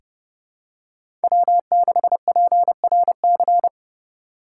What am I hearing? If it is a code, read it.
Morse code, heard as a keyed tone, about 29 wpm, 700 Hz. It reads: W6PRC
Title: USA Call Signs - 2x Character Spacing - 30wpm